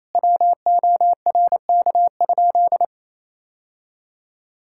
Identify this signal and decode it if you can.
Morse code, heard as a keyed tone, about 28 wpm, 700 Hz. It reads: WORK?
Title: Sentences from Top 1000 Words - Review Part 5 - 28wpm